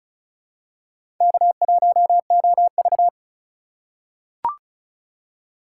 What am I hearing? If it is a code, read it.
Morse code, heard as a keyed tone, about 34 wpm, 700 Hz. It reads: K1OV
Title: USA Call Signs - 4 Characters - 2x Character Spacing - 35wpm